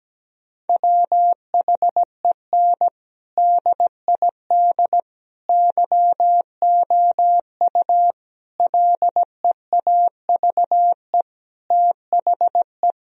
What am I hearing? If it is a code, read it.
Morse code, heard as a keyed tone, about 17 wpm, 700 Hz. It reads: WHEN DID YOU LEAVE THE
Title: Sentences from Top 1000 Words - Review Part 4 - 17wpm